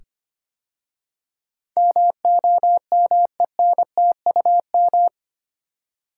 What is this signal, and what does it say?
Morse code, heard as a keyed tone, about 25 wpm, 700 Hz. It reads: MOMENTUM
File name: 8 Letter Words - First 3 Letters Given - 25wpm